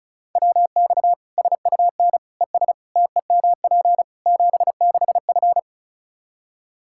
Morse code, heard as a keyed tone, about 34 wpm, 700 Hz. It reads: WX SUN ES TEMP 76F
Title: Practice QSO - Exchange 2 - Info - 35wpm